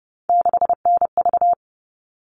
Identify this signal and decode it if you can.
Morse code, heard as a keyed tone, about 30 wpm, 700 Hz. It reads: 6NV
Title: All Cut Numbers - 3-Digit 30wpm